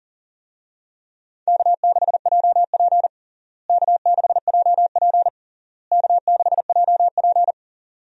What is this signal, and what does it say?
Morse code, heard as a keyed tone, about 37 wpm, 700 Hz. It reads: K6JP K6JP K6JP
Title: USA Call Signs - 4 Characters - Repeated 3x - 40wpm